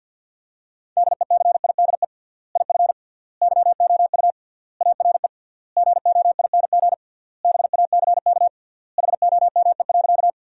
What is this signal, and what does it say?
Morse code, heard as a keyed tone, about 51 wpm, 700 Hz. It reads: DECIDE IF YOU ARE GOING BACK HOME.
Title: 3x Word-Spacing - Sentences from Top 500 Words - 50wpm